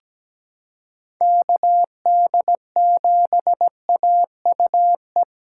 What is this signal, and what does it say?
Morse code, heard as a keyed tone, about 17 wpm, 700 Hz. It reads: KD7AUE
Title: USA Callsign DE Callsign - 6 Characters - 17wpm